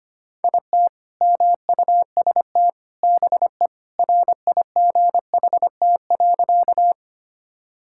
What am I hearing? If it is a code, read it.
Morse code, heard as a keyed tone, about 25 wpm, 700 Hz. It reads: IT MUST BE RIGHT.
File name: practice / 25wpm / Sentences with 2-Word Phrases - 25wpm